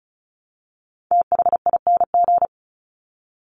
Morse code, heard as a keyed tone, about 34 wpm, 700 Hz. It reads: THING
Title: Top 300 Words 35wpm